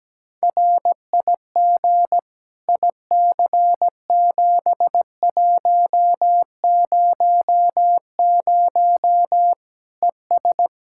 Morse code, heard as a keyed tone, about 17 wpm, 700 Hz. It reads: RIG IC7100 ES
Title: Practice QSO - Exchange 2 - Info - 17wpm